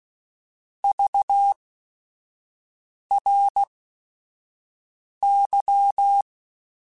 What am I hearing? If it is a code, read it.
Morse code, heard as a keyed tone, about 16 wpm, spaced as for 5 wpm, 800 Hz. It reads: VRY